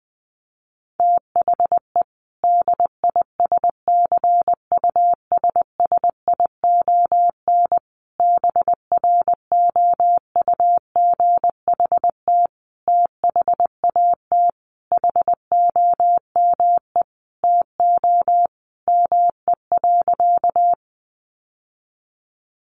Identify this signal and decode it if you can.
Morse code, heard as a keyed tone, about 19 wpm, 700 Hz. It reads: THE DISCUSSION BROUGHT THAT HOME TO ME.
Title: Sentences from Top 1100 Words 20wpm